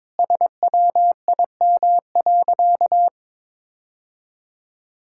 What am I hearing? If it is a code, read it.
Morse code, heard as a keyed tone, about 22 wpm, 700 Hz. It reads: SWIM.
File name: practice / 22wpm / Sentences with 4-Word Phrases - 22wpm